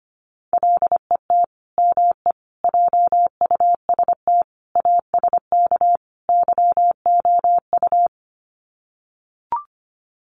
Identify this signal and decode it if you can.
Morse code, heard as a keyed tone, about 25 wpm, 700 Hz. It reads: LET ME JUST ASK YOU
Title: Sets of 5 Words Encore2 25wpm